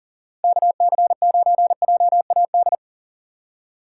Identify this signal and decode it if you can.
Morse code, heard as a keyed tone, about 40 wpm, 700 Hz. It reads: KC9JAD
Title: USA Call Signs - 6 Characters - 3x Character Spacing - 40wpm